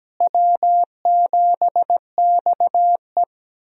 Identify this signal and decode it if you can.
Morse code, heard as a keyed tone, about 17 wpm, 700 Hz. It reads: W7XE
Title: USA Call Signs - 4 Characters - Repeated 3x - 17wpm